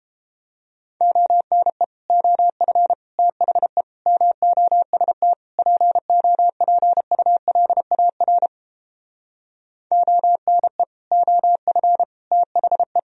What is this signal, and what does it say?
Morse code, heard as a keyed tone, about 31 wpm, 700 Hz. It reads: ONE OF THE MOST POPULAR ONE OF THE